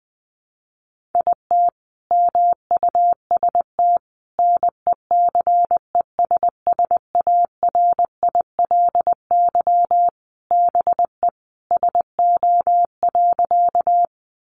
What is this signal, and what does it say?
Morse code, heard as a keyed tone, about 20 wpm, 700 Hz. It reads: IT MUST NECESSARILY BE SO.
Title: Sentences from Top 1400 Words 20wpm